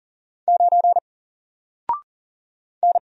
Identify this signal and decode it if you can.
Morse code, heard as a keyed tone, about 40 wpm, 700 Hz. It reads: 9 N
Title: Single Letter-Number Alt 40wpm